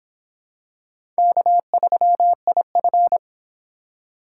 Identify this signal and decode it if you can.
Morse code, heard as a keyed tone, about 26 wpm, 700 Hz. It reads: K3IF